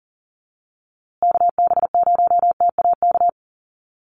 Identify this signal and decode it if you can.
Morse code, heard as a keyed tone, about 40 wpm, 700 Hz. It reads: KB0TAK
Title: USA Call Signs Repeated 3x 40wpm